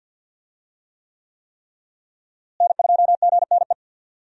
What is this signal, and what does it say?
Morse code, heard as a keyed tone, about 51 wpm, 700 Hz. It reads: N2GNE
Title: USA Call Signs - 5 Characters - Repeated 3x - 50wpm